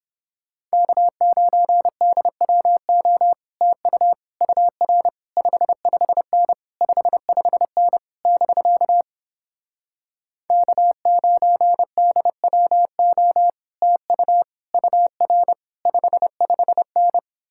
Morse code, heard as a keyed tone, about 31 wpm, 700 Hz. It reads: K9DWO TU UR 55N 55N <BK> K9DWO TU UR 55N